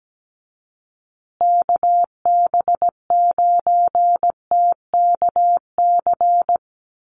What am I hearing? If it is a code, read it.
Morse code, heard as a keyed tone, about 17 wpm, 700 Hz. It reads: KB9TKC
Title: USA Call Signs Repeated 2x 17wpm